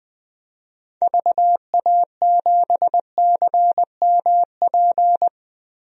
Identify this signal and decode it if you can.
Morse code, heard as a keyed tone, about 20 wpm, 700 Hz. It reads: VA7CMP
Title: DXCC-100 Call Signs - 6 Characters - 20wpm